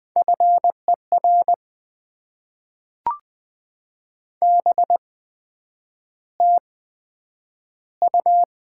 Morse code, heard as a keyed tone, about 20 wpm, spaced as for 9 wpm, 700 Hz. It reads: FER BTU
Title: Top 20 QSO Elements - 8x Character Spacing - 20wpm